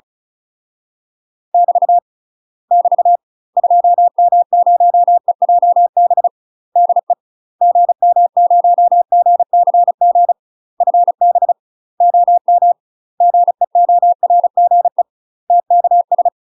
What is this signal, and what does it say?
Morse code, heard as a keyed tone, about 34 wpm, 700 Hz. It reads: <BT> <BT> 2M0EJB DE GM0GCG FB OM GEORGE TKS
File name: UK MASTER QSOs TRAINING / 35WPM / UK MASTER QSOs 0010